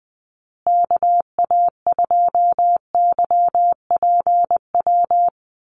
Morse code, heard as a keyed tone, about 20 wpm, 700 Hz. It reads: KA2YPW
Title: USA Call Signs Repeated 3x 20wpm